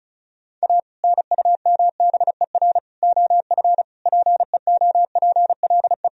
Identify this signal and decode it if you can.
Morse code, heard as a keyed tone, about 36 wpm, 700 Hz. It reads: A NUMBER OF PEOPLE
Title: Sentences from Top 1000 Words - Review Part 1 - 35wpm